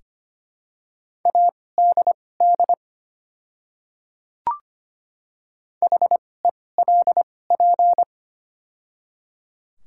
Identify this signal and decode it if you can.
Morse code, heard as a keyed tone, about 25 wpm, spaced as for 18 wpm, 690 Hz. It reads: ADD HELP
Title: 2x Character Spacing - Top 200 Words - 25wpm